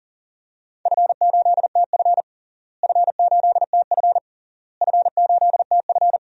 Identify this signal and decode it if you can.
Morse code, heard as a keyed tone, about 40 wpm, 700 Hz. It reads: F8TF F8TF F8TF
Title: DXCC-100 Call Signs - 4 Characters - Repeated 3x - 40wpm